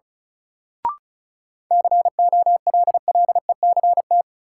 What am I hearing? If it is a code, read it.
Morse code, heard as a keyed tone, about 36 wpm, 700 Hz. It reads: COLLECT
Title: Sentences from Top 1600 Words - New Word Drill - 35wpm